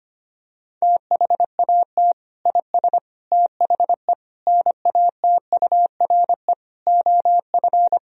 Morse code, heard as a keyed tone, about 25 wpm, 700 Hz. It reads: THAT IS THE NATURE OF
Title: Sentences from Top 1000 Words - Review Part 3 - 25wpm